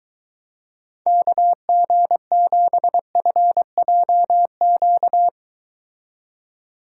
Morse code, heard as a keyed tone, about 23 wpm, 700 Hz. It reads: KG7FJQ